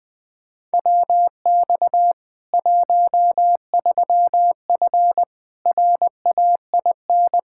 Morse code, heard as a keyed tone, about 21 wpm, 700 Hz. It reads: WX 13F RAIN
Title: Practice QSO - Exchange 2 - Info - 20wpm